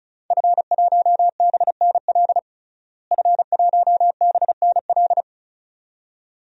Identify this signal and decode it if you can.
Morse code, heard as a keyed tone, about 36 wpm, 700 Hz. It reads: F1BNL F1BNL
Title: DXCC-100 Call Signs - 5 Characters - Repeated 2x - 35wpm